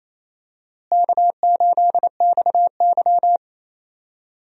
Morse code, heard as a keyed tone, about 28 wpm, 700 Hz. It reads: K8XY
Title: USA Call Signs - 4 Characters - Repeated 3x - 28wpm